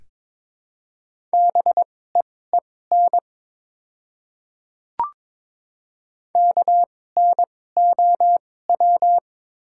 Morse code, heard as a keyed tone, about 22 wpm, spaced as for 16 wpm, 690 Hz. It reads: BEEN KNOW
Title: 2x Character Spacing - Top 100 Words - 22wpm